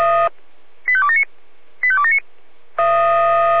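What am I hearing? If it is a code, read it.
Five-tone selective call (ZVEI), heard as quick runs of five tones, tones of about 80 ms. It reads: 85279 85279
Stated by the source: ZVEI five-tone selective call received on an SDR.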